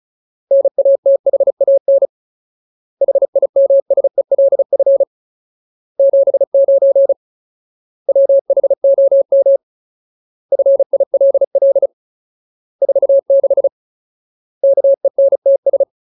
Morse code, heard as a keyed tone, about 34 wpm, 550 Hz. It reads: NATHAN HIMSELF 79 WHOM FILL 46 KENTS
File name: Ultimate Word List / 35WPM Ultimate